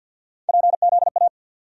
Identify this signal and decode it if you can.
Morse code, heard as a keyed tone, about 50 wpm, 700 Hz. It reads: PZA
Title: USA Call Sign Postfixes - 3 Letters 50wpm